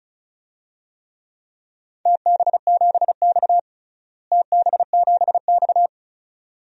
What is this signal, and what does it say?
Morse code, heard as a keyed tone, about 34 wpm, 700 Hz. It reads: TB7X TB7X
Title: DXCC-100 Call Signs - 4 Characters - Repeated 2x - 35wpm